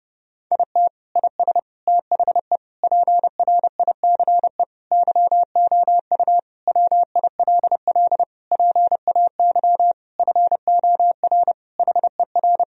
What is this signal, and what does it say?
Morse code, heard as a keyed tone, about 30 wpm, 700 Hz. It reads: IT IS THE PRICE YOU WILL PAY FOR HER
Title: Sentences from Top 1000 Words - Review Part 4 - 30wpm